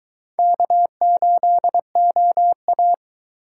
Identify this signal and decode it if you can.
Morse code, heard as a keyed tone, about 22 wpm, 700 Hz. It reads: K8OA